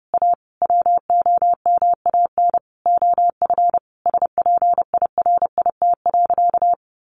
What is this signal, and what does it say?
Morse code, heard as a keyed tone, about 30 wpm, 700 Hz. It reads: A WOMAN OF SPIRIT.
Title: Sentences from Top 1400 Words 30wpm